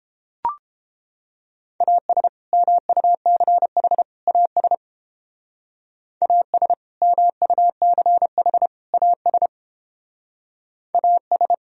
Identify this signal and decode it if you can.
Morse code, heard as a keyed tone, about 31 wpm, 700 Hz. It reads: AS MUCH AS AS MUCH AS AS